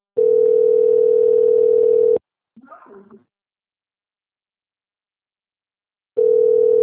Ringback tone (North America), 440 + 480 Hz.